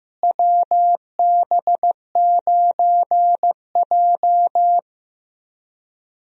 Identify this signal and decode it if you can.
Morse code, heard as a keyed tone, about 15 wpm, 700 Hz. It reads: WB9J
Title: USA Call Signs - 4 Characters - Repeated 3x - 15wpm